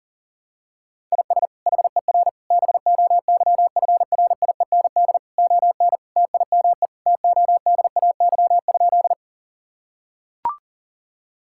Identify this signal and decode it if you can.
Morse code, heard as a keyed tone, about 40 wpm, 700 Hz. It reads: IS HER BOYFRIEND ON TIME TODAY?
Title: Sentences from Top 1000 Words - Review Part 5 - 40wpm